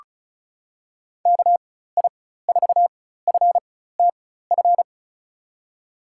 Morse code, heard as a keyed tone, about 34 wpm, spaced as for 16 wpm, 700 Hz. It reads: KI4FTF
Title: USA Call Signs - 6 Characters - 4x Character Spacing - 35wpm